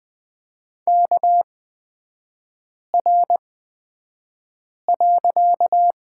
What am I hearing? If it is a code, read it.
Morse code, heard as a keyed tone, about 20 wpm, spaced as for 5 wpm, 700 Hz. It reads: KR.
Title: 255 Lesson 41e - Period - Three Characters 20wpm